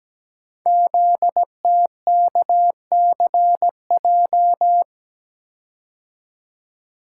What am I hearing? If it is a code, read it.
Morse code, heard as a keyed tone, about 17 wpm, 700 Hz. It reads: ZTKCJ
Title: Code Groups - 5 Letters - 17wpm